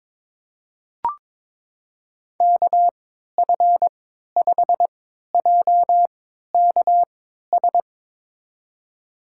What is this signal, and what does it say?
Morse code, heard as a keyed tone, about 22 wpm, spaced as for 12 wpm, 700 Hz. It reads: KF5JKS
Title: USA Call Signs - 6 Characters - 3x Character Spacing - 22wpm